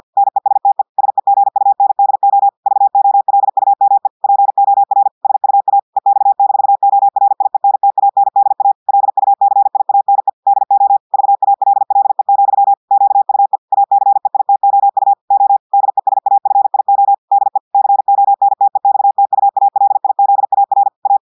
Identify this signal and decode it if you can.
Morse code, heard as a keyed tone, about 50 wpm, 800 Hz. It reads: NESTE SEGUNDO VOLUME POR SUA EX-ORIENTANDA LUCIANE DO VALLE, QUE ACEITOU O DESAFIO DE CONTEXTUALIZAR A
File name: audios / portugais / portugais texte en clair 1 vitesse 50 mots minutes